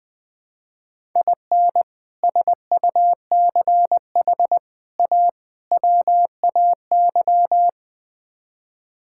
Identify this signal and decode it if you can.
Morse code, heard as a keyed tone, about 21 wpm, 700 Hz. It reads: IN SUCH A WAY
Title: Build A Phrase - 5-Word Phrases - 20wpm